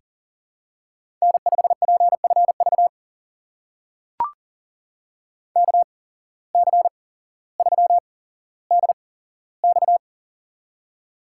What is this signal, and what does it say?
Morse code, heard as a keyed tone, about 39 wpm, spaced as for 17 wpm, 700 Hz. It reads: N5PFV KC3DX
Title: USA Call Signs - 5 Characters - 8x Character Spacing - 40wpm